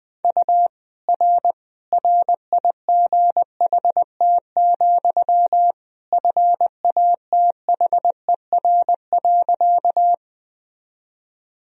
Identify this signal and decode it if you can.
Morse code, heard as a keyed tone, about 20 wpm, 700 Hz. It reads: U R RIGHT, FATHER.
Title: Sentences from Top 300 Words and Abbreviations 20wpm